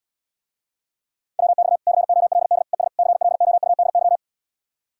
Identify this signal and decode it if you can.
Morse code, heard as a keyed tone, about 80 wpm, 700 Hz. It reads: WX COLD ES CLOUDY